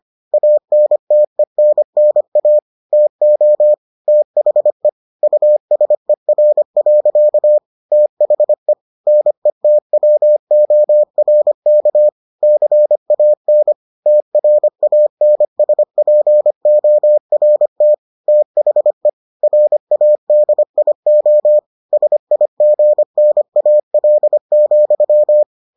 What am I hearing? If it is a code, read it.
Morse code, heard as a keyed tone, about 25 wpm, 600 Hz. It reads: ANTENNA TO THE USER. THE NETWORK CAN TRANSPORT THE RADIO SIGNAL,